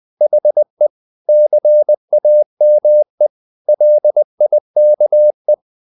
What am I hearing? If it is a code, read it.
Morse code, heard as a keyed tone, about 20 wpm, 600 Hz. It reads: HE CAME LIKE
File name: the old man and the sea part 2 20wpm